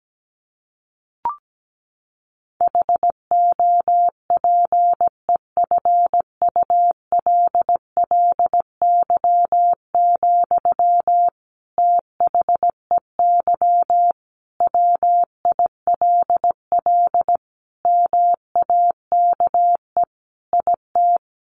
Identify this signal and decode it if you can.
Morse code, heard as a keyed tone, about 17 wpm, 700 Hz. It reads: HOPEFULLY, THEY WILL MAKE IT
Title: Sentences from Top 1500 Words 17wpm